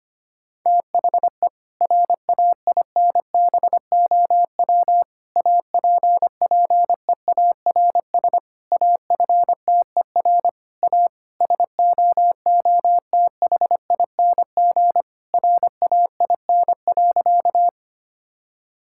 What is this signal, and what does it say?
Morse code, heard as a keyed tone, about 25 wpm, 700 Hz. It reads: THE RAINBOW APPEARS AFTER A SOOTHING RAIN.
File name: practice / 25wpm / Difficult Sentences with Hints - 25wpm